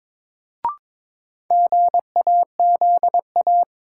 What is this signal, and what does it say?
Morse code, heard as a keyed tone, about 21 wpm, 700 Hz. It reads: GAZA
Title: Sets of 5 Words - New Word Drill - 22wpm